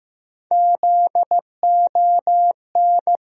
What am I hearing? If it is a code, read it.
Morse code, heard as a keyed tone, about 15 wpm, 700 Hz. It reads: ZON
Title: USA Call Sign Postfixes - 3 Letters 15wpm